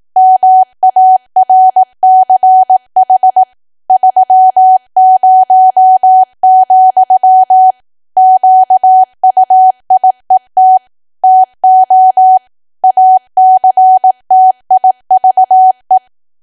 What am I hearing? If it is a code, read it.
Morse code, heard as a keyed tone, about 18 wpm, 750 Hz. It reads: MARCH 30, QUIET TO ACTIVE
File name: cwa / W1AW-Level2 / 18WPM